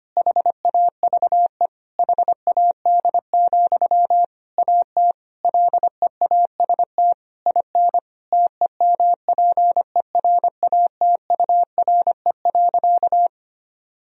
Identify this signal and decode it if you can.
Morse code, heard as a keyed tone, about 25 wpm, 700 Hz. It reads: HAVE HAD, AT LEAST IN TEMPERATURE.
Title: Sentences from Top 1000 Words - Review Part 4 - 25wpm